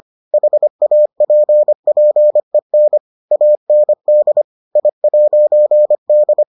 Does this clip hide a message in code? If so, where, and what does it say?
Yes: Morse code from 0.3 to 6.4 s, heard as a keyed tone, about 25 wpm, 600 Hz. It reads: HAPPEN AND I'D